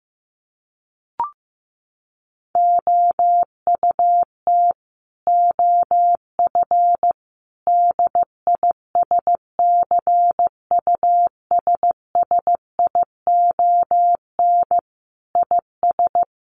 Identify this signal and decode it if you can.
Morse code, heard as a keyed tone, about 15 wpm, 700 Hz. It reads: OUT OF DISCUSSION IS
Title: Sentences from Top 1100 Words 15wpm